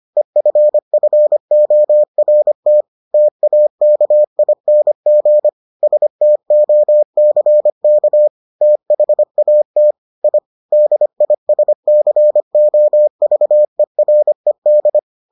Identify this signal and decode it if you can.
Morse code, heard as a keyed tone, about 25 wpm, 600 Hz. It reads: EFFORT TAKING STOCK THAT I DISCOVERED